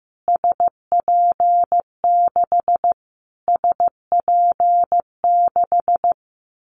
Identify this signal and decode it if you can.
Morse code, heard as a keyed tone, about 15 wpm, 700 Hz. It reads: SP6 SP6
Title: Callsign Prefixes for Top 25 DXCC - Repeated 3x - 15wpm